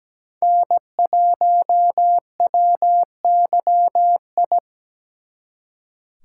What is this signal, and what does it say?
Morse code, heard as a keyed tone, about 20 wpm, 700 Hz. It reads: N1WYI
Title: Speed Racing - USA Call Signs 17wpm